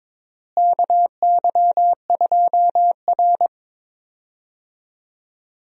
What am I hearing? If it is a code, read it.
Morse code, heard as a keyed tone, about 22 wpm, 700 Hz. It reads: KY2R
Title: USA Call Signs - 4 Characters - Repeated 3x - 22wpm